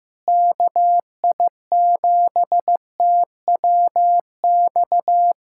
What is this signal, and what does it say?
Morse code, heard as a keyed tone, about 15 wpm, 700 Hz. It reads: KI7TWX
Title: USA Callsign DE Callsign - 6 Characters - 15wpm